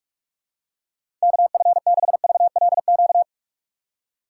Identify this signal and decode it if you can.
Morse code, heard as a keyed tone, about 41 wpm, 700 Hz. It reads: KF6VLQ
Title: USA Call Signs - 6 Characters - 3x Character Spacing - 45wpm